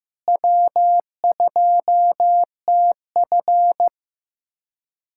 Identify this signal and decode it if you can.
Morse code, heard as a keyed tone, about 15 wpm, 700 Hz. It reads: W2TF
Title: USA Callsign DE Callsign - 4 Characters - 15wpm